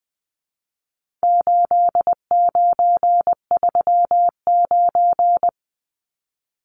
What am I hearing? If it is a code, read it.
Morse code, heard as a keyed tone, about 20 wpm, 700 Hz. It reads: 8939
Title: Numbers - 4-Digit 20wpm